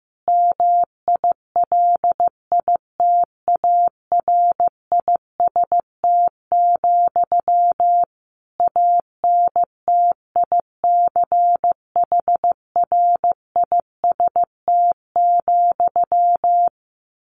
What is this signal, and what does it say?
Morse code, heard as a keyed tone, about 14 wpm, 700 Hz. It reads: MILITARIST, ANTICHRIST,
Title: A Sound Of Thunder 15wpm